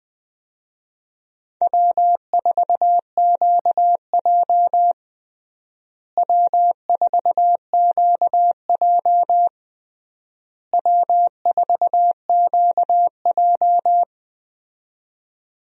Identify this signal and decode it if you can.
Morse code, heard as a keyed tone, about 20 wpm, 700 Hz. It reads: W4QJ W4QJ W4QJ
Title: USA Call Signs - 4 Characters - Repeated 3x - 20wpm